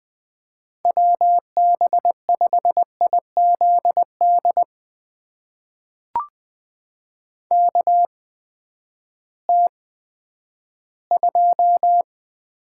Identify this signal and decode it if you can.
Morse code, heard as a keyed tone, about 20 wpm, spaced as for 11 wpm, 700 Hz. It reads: WB5IZD KT2
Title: USA Call Signs - 6 Characters - 8x Character Spacing - 20wpm